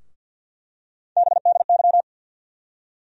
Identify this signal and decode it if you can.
Morse code, heard as a keyed tone, about 52 wpm, 700 Hz. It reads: BDY
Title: USA Call Sign Postfixes - 3 Letters 50wpm